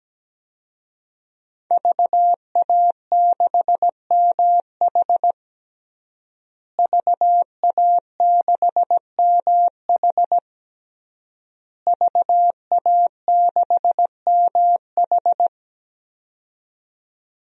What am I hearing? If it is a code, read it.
Morse code, heard as a keyed tone, about 17 wpm, 700 Hz. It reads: VA6MH VA6MH VA6MH
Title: DXCC-100 Call Signs - 5 Characters - Repeated 3x - 17wpm